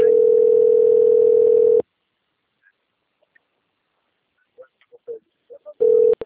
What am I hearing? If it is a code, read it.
Ringback tone (North America), 440 + 480 Hz.